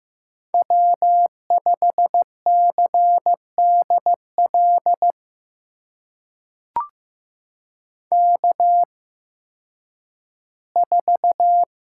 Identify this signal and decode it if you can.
Morse code, heard as a keyed tone, about 15 wpm, spaced as for 10 wpm, 700 Hz. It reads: W5CDL K4
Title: USA Call Signs - 5 Characters - 8x Character Spacing - 15wpm